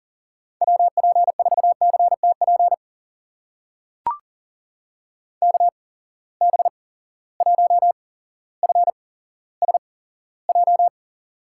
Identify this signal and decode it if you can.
Morse code, heard as a keyed tone, about 38 wpm, spaced as for 17 wpm, 700 Hz. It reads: WP4CTP KB1FSJ
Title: USA Call Signs - 6 Characters - 8x Character Spacing - 40wpm